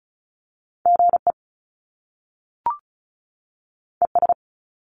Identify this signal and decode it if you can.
Morse code, heard as a keyed tone, about 35 wpm, 700 Hz. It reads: GE ES
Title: Top Two Letter Combinations in Words 35wpm